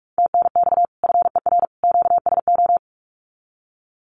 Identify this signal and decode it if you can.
Morse code, heard as a keyed tone, about 44 wpm, 700 Hz. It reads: TNX FER QSO
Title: Common QSO Phrases 45wpm